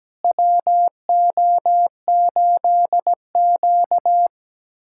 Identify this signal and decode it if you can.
Morse code, heard as a keyed tone, about 17 wpm, 700 Hz. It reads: WO8Q
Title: USA Callsign DE Callsign - 4 Characters - 17wpm